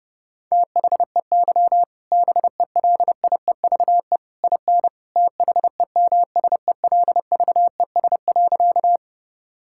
Morse code, heard as a keyed tone, about 30 wpm, 700 Hz. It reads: THEY BELIEVE IN THEMSELVES.
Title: Sentences from Top 1000 Words - Review Part 5 - 30wpm